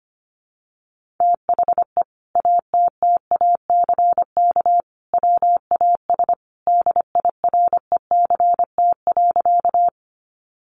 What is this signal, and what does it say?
Morse code, heard as a keyed tone, about 25 wpm, 700 Hz. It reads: THE ATTACK WAS DIRECT.
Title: Sentences from Top 1500 Words 25wpm